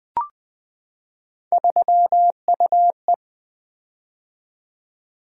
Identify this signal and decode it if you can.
Morse code, heard as a keyed tone, about 20 wpm, 700 Hz. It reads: 3UE
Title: Three Letter-Number Combinations - 20wpm